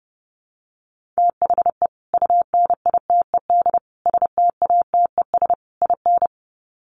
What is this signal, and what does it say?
Morse code, heard as a keyed tone, about 30 wpm, 700 Hz. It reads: THE UNITED STATES IN
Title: Sets of 4 Words 30wpm